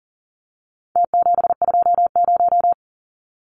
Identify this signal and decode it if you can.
Morse code, heard as a keyed tone, about 40 wpm, 700 Hz. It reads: T720
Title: Numbers - Contest Style - 4-Digit 40wpm